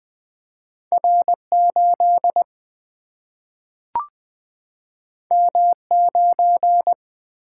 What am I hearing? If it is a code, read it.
Morse code, heard as a keyed tone, about 20 wpm, 700 Hz. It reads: R 8 M 9
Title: Single-Letter Single-Number 20wpm